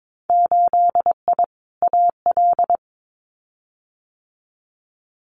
Morse code, heard as a keyed tone, about 22 wpm, 700 Hz. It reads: 8I AL
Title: Winter Field Day Exchanges Encore 22wpm